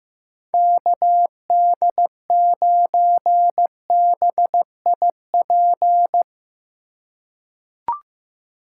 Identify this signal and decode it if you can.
Morse code, heard as a keyed tone, about 15 wpm, 700 Hz. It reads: KD9BIP
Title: USA Call Signs - 6 Characters - 3x Character Spacing - 15wpm